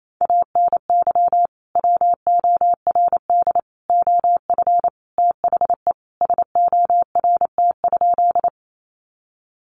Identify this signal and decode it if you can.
Morse code, heard as a keyed tone, about 28 wpm, 700 Hz. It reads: ANY WORD OF THE SORT?
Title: Sentences from Top 1300 Words 28wpm